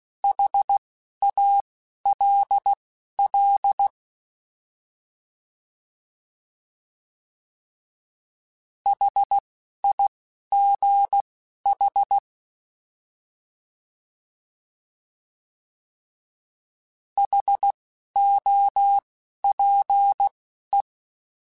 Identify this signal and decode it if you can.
Morse code, heard as a keyed tone, about 16 wpm, spaced as for 11 wpm, 800 Hz. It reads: HALL HIGH HOPE
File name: Alphabetical / Alphabetical mp3s / Hs @10